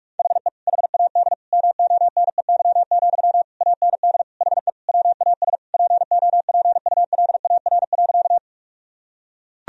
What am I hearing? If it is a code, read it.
Morse code, heard as a keyed tone, about 43 wpm, 700 Hz. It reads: HE HAD MONEY, AND HE WAS POPULAR.
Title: Sentences from Top 1000 Words - Review Part 3 - 45wpm